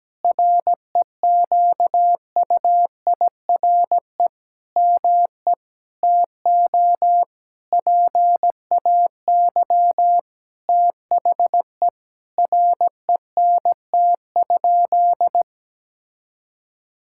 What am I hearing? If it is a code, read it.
Morse code, heard as a keyed tone, about 17 wpm, 700 Hz. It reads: REQUIRE ME TO PAY THE RENT?
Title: Sentences from Top 1000 Words - Review Part 6 - 17wpm